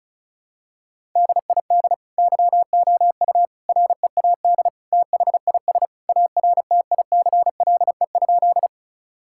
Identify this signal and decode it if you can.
Morse code, heard as a keyed tone, about 36 wpm, 700 Hz. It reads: DID YOU READ THIS ARTICLE?
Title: Sentences from Top 1000 Words - Review Part 5 - 35wpm